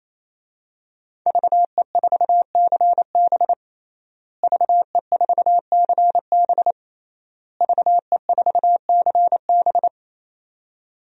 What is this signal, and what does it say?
Morse code, heard as a keyed tone, about 28 wpm, 700 Hz. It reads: VE4CB VE4CB VE4CB
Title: DXCC-100 Call Signs - 5 Characters - Repeated 3x - 28wpm